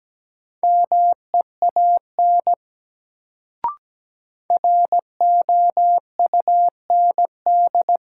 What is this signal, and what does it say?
Morse code, heard as a keyed tone, about 17 wpm, 700 Hz. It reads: MEAN ROUND
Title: Sentences from Top 200 Words - New Word Drill - 17wpm